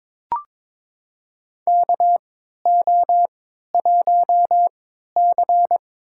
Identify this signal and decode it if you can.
Morse code, heard as a keyed tone, about 22 wpm, spaced as for 13 wpm, 700 Hz. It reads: KO1C
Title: USA Call Signs - 4 Characters - 3x Character Spacing - 22wpm